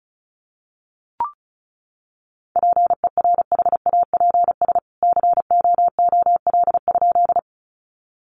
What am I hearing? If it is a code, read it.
Morse code, heard as a keyed tone, about 34 wpm, 700 Hz. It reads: PERHAPS COOL?
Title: Sentences from Top 1200 Words 35wpm